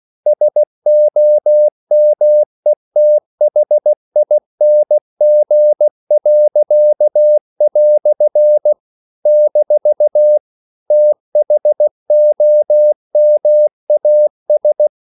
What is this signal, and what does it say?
Morse code, heard as a keyed tone, about 16 wpm, 600 Hz. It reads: SOMETHING." - THOMAS
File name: CW / Practice / EnglishQuotations0116